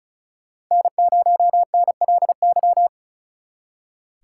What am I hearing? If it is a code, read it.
Morse code, heard as a keyed tone, about 45 wpm, 700 Hz. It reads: N0NLY